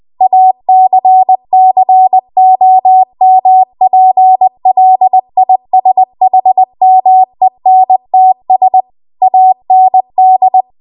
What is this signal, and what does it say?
Morse code, heard as a keyed tone, about 20 wpm, 750 Hz. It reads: ACCOMPLISHMENTS AND
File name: Morse / 20WPM